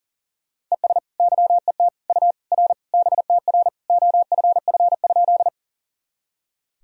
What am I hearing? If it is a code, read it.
Morse code, heard as a keyed tone, about 40 wpm, 700 Hz. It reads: ES YET U R BTR OFF?
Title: Sentences from Top 500 Words and Abbreviations 40wpm